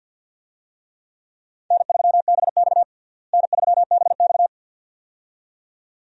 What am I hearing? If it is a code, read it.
Morse code, heard as a keyed tone, about 51 wpm, 700 Hz. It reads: N3BX N3BX
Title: USA Call Signs - 4 Characters - Repeated 2x - 50wpm